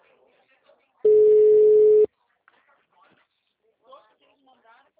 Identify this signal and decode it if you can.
Ringback tone (Europe, 425 Hz), with speech.